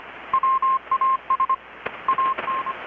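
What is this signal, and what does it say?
Morse code, heard as a keyed tone, about 25 wpm, 1060 Hz. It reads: WAS AN
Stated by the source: Morse from KPH's Night of Nights XX maritime radiotelegraphy event, received in Chiapas, Mexico about 3,800 km away on a Kenwood R-600 with a grounded 10 m long-wire antenna.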